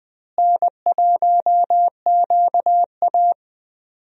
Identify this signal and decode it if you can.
Morse code, heard as a keyed tone, about 20 wpm, 700 Hz. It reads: N1QA
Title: USA Call Signs - 4 Characters - Repeated 2x - 20wpm